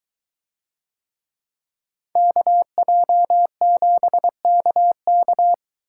Morse code, heard as a keyed tone, about 22 wpm, 700 Hz. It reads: KJ7KK